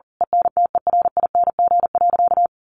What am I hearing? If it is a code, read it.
Morse code, heard as a keyed tone, about 40 wpm, 700 Hz. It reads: ENTERING.
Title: A Sound Of Thunder 40wpm